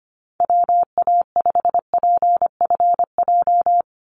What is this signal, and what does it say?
Morse code, heard as a keyed tone, about 25 wpm, 700 Hz. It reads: WA5PFJ
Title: Call Signs Repeated 3x - 25wpm